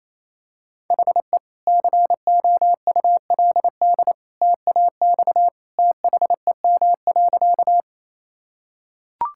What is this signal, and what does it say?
Morse code, heard as a keyed tone, about 28 wpm, 700 Hz. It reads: HE COULD TAX THEM.
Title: Sentences from Top 1000 Words - Review Part 5 - 28wpm